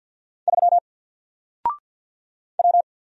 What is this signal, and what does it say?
Morse code, heard as a keyed tone, about 52 wpm, 700 Hz. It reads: 3 W
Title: Single Character (Letter:Number:Punctuation:Prosign) Alt 50wpm